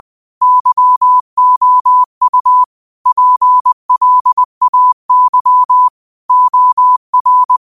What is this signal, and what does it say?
Morse code, heard as a keyed tone, about 20 wpm, 1000 Hz. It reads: YOU PLAY OR